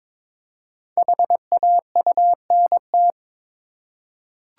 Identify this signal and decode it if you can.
Morse code, heard as a keyed tone, about 22 wpm, 700 Hz. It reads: HAUNT
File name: 5 Letter Words - First 3 Letters Given - 22wpm